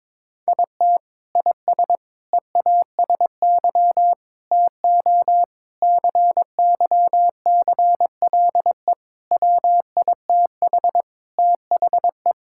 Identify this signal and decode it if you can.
Morse code, heard as a keyed tone, about 22 wpm, 700 Hz. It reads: IT IS EASY TO CYCLE WITH THE
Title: Sentences from Top 1000 Words - Review Part 6 - 22wpm